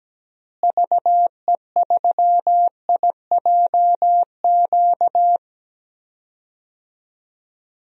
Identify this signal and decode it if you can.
Morse code, heard as a keyed tone, about 17 wpm, 700 Hz. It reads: VE3IJQ
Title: DXCC-100 Call Signs - 6 Characters - Repeated 2x - 17wpm